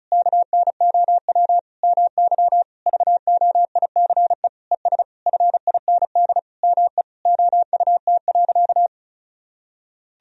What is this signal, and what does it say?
Morse code, heard as a keyed tone, about 36 wpm, 700 Hz. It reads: KNOW MY VOICE ES FIND ME OUT.
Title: Sentences from Top 500 Words and Abbreviations 35wpm